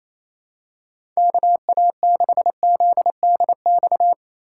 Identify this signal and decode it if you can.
Morse code, heard as a keyed tone, about 28 wpm, 700 Hz. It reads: KA6ZDX
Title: USA Call Signs - 6 Characters - 2x Character Spacing - 28wpm